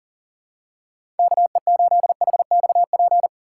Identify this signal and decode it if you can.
Morse code, heard as a keyed tone, about 40 wpm, 700 Hz. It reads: KE8HXP
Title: USA Call Signs - 6 Characters - 4x Character Spacing - 40wpm